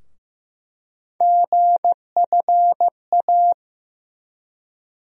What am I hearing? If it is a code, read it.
Morse code, heard as a keyed tone, about 15 wpm, 700 Hz. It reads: GFA
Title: USA Call Sign Postfixes - 3 Letters 15wpm